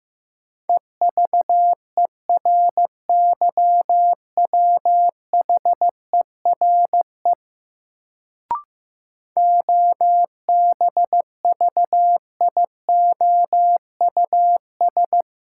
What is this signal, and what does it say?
Morse code, heard as a keyed tone, about 15 wpm, 700 Hz. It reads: EVERYWHERE OBVIOUS
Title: Sentences from Top 1600 Words - New Word Drill - 15wpm